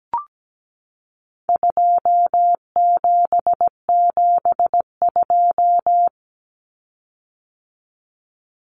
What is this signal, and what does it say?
Morse code, heard as a keyed tone, about 17 wpm, 700 Hz. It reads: 2772
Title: Common Cut Numbers - 4-Digit 17wpm